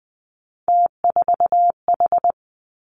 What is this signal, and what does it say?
Morse code, heard as a keyed tone, about 20 wpm, 700 Hz. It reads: T4H
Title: Three Challenging Letter-Number Combinations 20wpm